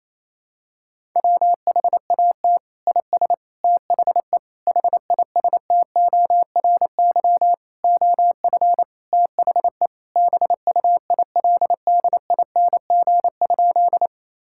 Morse code, heard as a keyed tone, about 28 wpm, 700 Hz. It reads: WHAT IS THE HISTORY OF THE BUILDING?
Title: Sentences from Top 1000 Words - Review Part 5 - 28wpm